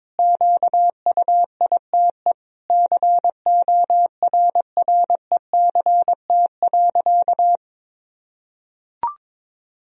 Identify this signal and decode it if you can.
Morse code, heard as a keyed tone, about 22 wpm, 700 Hz. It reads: QUITE CORRECT.
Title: Sentences from Top 1000 Words - Review Part 2 - 22wpm